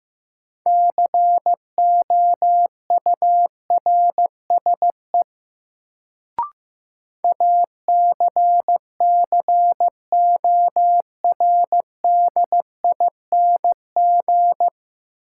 Morse code, heard as a keyed tone, about 15 wpm, 700 Hz. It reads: COURSE ACCORDING
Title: Sets of 2 Words - New Word Drill - 15wpm